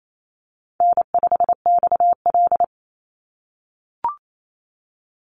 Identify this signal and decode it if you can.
Morse code, heard as a keyed tone, about 28 wpm, 700 Hz. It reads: N5XL
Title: NA SOTA Chaser Callsigns 28wpm